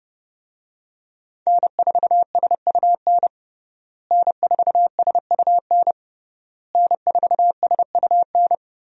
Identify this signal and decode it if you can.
Morse code, heard as a keyed tone, about 30 wpm, 700 Hz. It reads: N4SUN N4SUN N4SUN
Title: USA Call Signs - 5 Characters - Repeated 3x - 30wpm